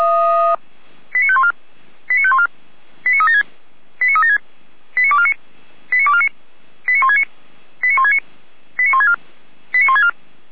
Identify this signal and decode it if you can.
Five-tone selective call (ZVEI), heard as quick runs of five tones, tones of about 70 ms. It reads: 89514 89514 89376 89376 89249 89249 89169 89169 89164 89164